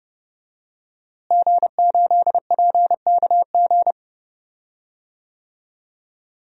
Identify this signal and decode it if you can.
Morse code, heard as a keyed tone, about 30 wpm, 700 Hz. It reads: G8PKG
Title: DXCC-100 Call Signs - 5 Characters - Repeated 3x - 30wpm